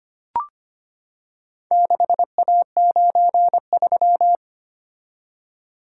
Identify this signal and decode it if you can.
Morse code, heard as a keyed tone, about 25 wpm, 700 Hz. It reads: 6A93
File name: practice / 25wpm / Four Letter-Number Combinations - 25wpm